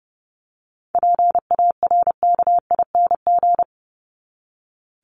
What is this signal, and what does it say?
Morse code, heard as a keyed tone, about 30 wpm, 700 Hz. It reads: PARKING
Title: Top 1501-1750 Words 30wpm